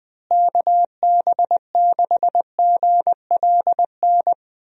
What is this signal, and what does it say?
Morse code, heard as a keyed tone, about 20 wpm, 700 Hz. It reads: KB6GLN
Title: USA Call Signs - 6 Characters - Repeated 3x - 20wpm